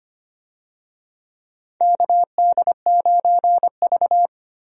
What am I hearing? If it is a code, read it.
Morse code, heard as a keyed tone, about 25 wpm, 700 Hz. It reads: KD9V
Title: USA Call Signs - 4 Characters - Repeated 2x - 25wpm